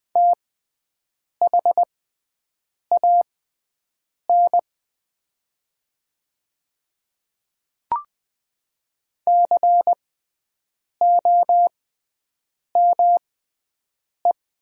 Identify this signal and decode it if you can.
Morse code, heard as a keyed tone, about 20 wpm, spaced as for 7 wpm, 700 Hz. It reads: THAN COME
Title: 6x Character Spacing - Top 100 Words - 20wpm